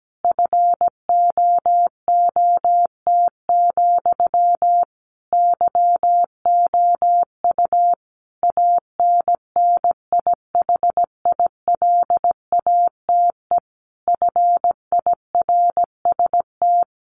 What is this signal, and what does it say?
Morse code, heard as a keyed tone, about 17 wpm, 700 Hz. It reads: FOOT, YOU ANNIHILATE FIRST
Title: A Sound Of Thunder 17wpm